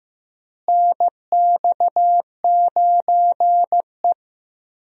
Morse code, heard as a keyed tone, about 15 wpm, 700 Hz. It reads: NX9E
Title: USA Call Signs - 4 Characters - Repeated 3x - 15wpm